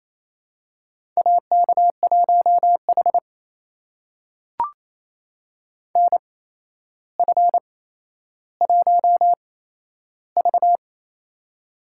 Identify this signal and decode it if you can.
Morse code, heard as a keyed tone, about 28 wpm, spaced as for 12 wpm, 700 Hz. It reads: AK1H NF1V
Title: USA Call Signs - 4 Characters - 8x Character Spacing - 28wpm